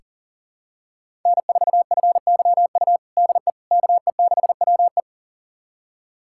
Morse code, heard as a keyed tone, about 40 wpm, 700 Hz. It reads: N4FYU DE KE6WE
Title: USA Callsign DE Callsign - 5 Characters - 40wpm